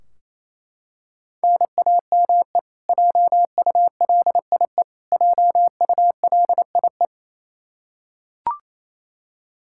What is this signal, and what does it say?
Morse code, heard as a keyed tone, about 28 wpm, 700 Hz. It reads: NAME JULIE JULIE
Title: Practice QSO - Exchange 1 - Beginning - 28wpm